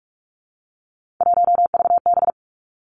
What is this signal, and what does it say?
Morse code, heard as a keyed tone, about 46 wpm, 700 Hz. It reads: 1VB
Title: Three Challenging Letter-Number Combinations 45wpm